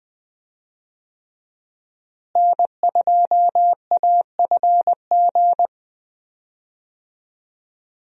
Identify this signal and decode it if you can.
Morse code, heard as a keyed tone, about 21 wpm, 700 Hz. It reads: N2AFG
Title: USA Call Signs - 5 Characters - Repeated 3x - 20wpm